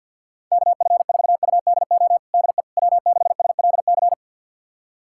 Morse code, heard as a keyed tone, about 52 wpm, 700 Hz. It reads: KF4UDO DE W6SLC
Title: USA Callsign DE Callsign 50wpm